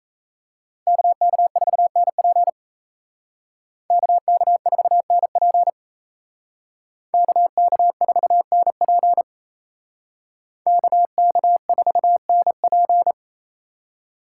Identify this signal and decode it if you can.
Morse code, heard as a keyed tone, about 35 wpm, 700 Hz. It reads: KK4NP KK4NP KK4NP KK4NP